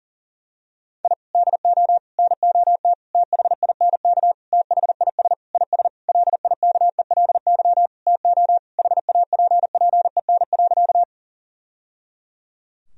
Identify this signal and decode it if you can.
Morse code, heard as a keyed tone, about 40 wpm, 700 Hz. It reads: I DO NOT THINK THIS IS LIKELY TO HAPPEN.
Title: Sentences from Top 1000 Words - Review Part 4 - 40wpm